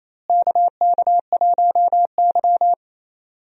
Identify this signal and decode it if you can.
Morse code, heard as a keyed tone, about 28 wpm, 700 Hz. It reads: KK1Y
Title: USA Call Signs - 4 Characters - Repeated 2x - 28wpm